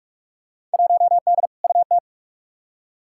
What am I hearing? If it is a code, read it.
Morse code, heard as a keyed tone, about 46 wpm, 700 Hz. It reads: 1D UT
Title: Field Day Exchanges - 45wpm